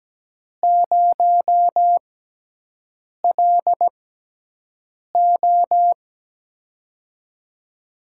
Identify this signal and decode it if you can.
Morse code, heard as a keyed tone, about 17 wpm, spaced as for 6 wpm, 700 Hz. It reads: 0LO
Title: USA Call Signs - 4 Characters - 6x Character Spacing - 17wpm